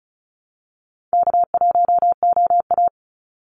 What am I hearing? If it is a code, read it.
Morse code, heard as a keyed tone, about 34 wpm, 700 Hz. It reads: K1OA
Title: Easy Call Signs Repeated 3x - 35wpm